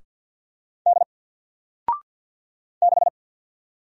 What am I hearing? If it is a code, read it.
Morse code, heard as a keyed tone, about 50 wpm, 700 Hz. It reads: D 6
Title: Single Letter-Number 50wpm